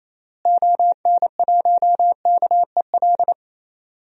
Morse code, heard as a keyed tone, about 28 wpm, 700 Hz. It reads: ON1KEL
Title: DXCC-100 Call Signs - 6 Characters - Repeated 3x - 28wpm